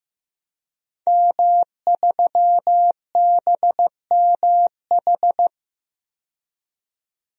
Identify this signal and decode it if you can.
Morse code, heard as a keyed tone, about 15 wpm, 700 Hz. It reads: M3BMH
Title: DXCC-100 Call Signs - 5 Characters - 15wpm